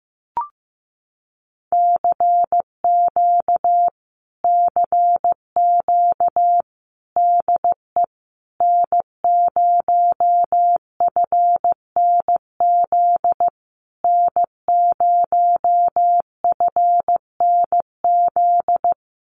Morse code, heard as a keyed tone, about 15 wpm, 700 Hz. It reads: CQ CQ DE N0FNZ N0FNZ
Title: Common QSO Phrases Encore 15wpm